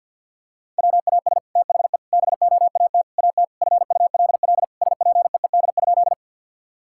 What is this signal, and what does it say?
Morse code, heard as a keyed tone, about 49 wpm, 700 Hz. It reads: WAS THE BOAT AT FULL SPEED?
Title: Sentences from Top 1000 Words - Review Part 5 - 50wpm